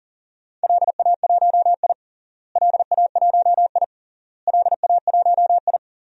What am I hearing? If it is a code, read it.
Morse code, heard as a keyed tone, about 40 wpm, 700 Hz. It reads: LA1I LA1I LA1I
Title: DXCC-100 Call Signs - 4 Characters - Repeated 3x - 40wpm